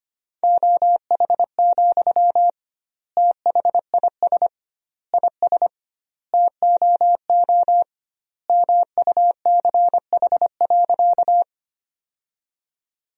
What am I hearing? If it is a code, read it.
Morse code, heard as a keyed tone, about 25 wpm, 700 Hz. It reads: OH, THIS IS TOO MUCH.
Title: 2x Word-Spacing - Sentences from Top 500 Words - 25wpm